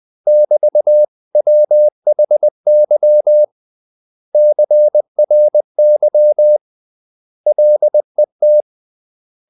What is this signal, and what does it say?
Morse code, heard as a keyed tone, about 20 wpm, 600 Hz. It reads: <BT>WHY CRY LET
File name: CW / Practice / 500MostCommonEnglishWords20